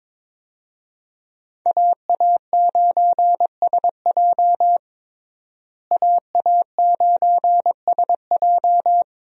Morse code, heard as a keyed tone, about 22 wpm, 700 Hz. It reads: AA9SJ AA9SJ
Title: USA Call Signs - 5 Characters - Repeated 2x - 22wpm